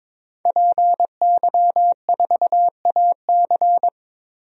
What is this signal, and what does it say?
Morse code, heard as a keyed tone, about 22 wpm, 700 Hz. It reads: PY4AC
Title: DXCC-100 Call Signs - 5 Characters - 22wpm